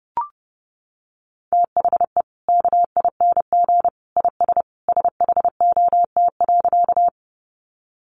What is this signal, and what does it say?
Morse code, heard as a keyed tone, about 30 wpm, 700 Hz. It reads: THE KING IS SHOT.
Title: Sentences from Top 1100 Words 30wpm